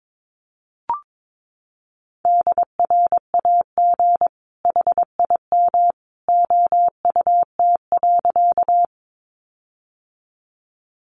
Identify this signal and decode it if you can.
Morse code, heard as a keyed tone, about 22 wpm, 700 Hz. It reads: DRAG HIM OUT.
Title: Sentences from Top 1500 Words 22wpm